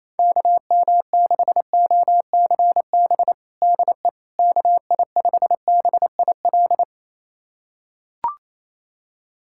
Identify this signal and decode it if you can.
Morse code, heard as a keyed tone, about 28 wpm, 700 Hz. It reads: KM6OCB DE KI5BIL
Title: USA Callsign DE Callsign - 6 Characters - 28wpm